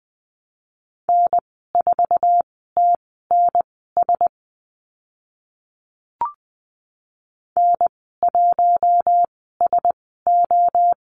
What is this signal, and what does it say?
Morse code, heard as a keyed tone, about 20 wpm, spaced as for 14 wpm, 700 Hz. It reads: N4TNS N1SO
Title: 048 Lesson 9g - Number 4 - Call Signs 20wpm